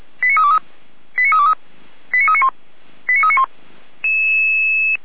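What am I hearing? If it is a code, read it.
Five-tone selective call (ZVEI), heard as quick runs of five tones, tones of about 70 ms. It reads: 89323 89323 89391 89391
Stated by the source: Five-tone ZVEI selective call received by SDR.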